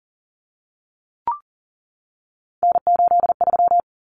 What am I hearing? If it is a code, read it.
Morse code, heard as a keyed tone, about 40 wpm, 700 Hz. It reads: N83
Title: Common Cut Numbers - 3-Digit 40wpm